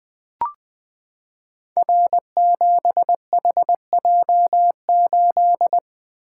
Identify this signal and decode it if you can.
Morse code, heard as a keyed tone, about 20 wpm, 700 Hz. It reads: R7HJ8
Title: Code Groups - 5 Letter:Numbers - 20wpm